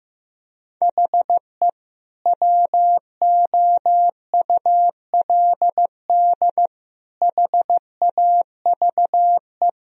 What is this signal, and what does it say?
Morse code, heard as a keyed tone, about 16 wpm, 700 Hz. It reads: HE WOULD HAVE
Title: Sentences from Top 1000 Words - Review Part 1 - 15wpm